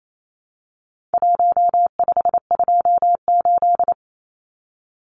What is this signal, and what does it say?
Morse code, heard as a keyed tone, about 28 wpm, 700 Hz. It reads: 1528
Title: Numbers - Contest Style - 4-Digit 28wpm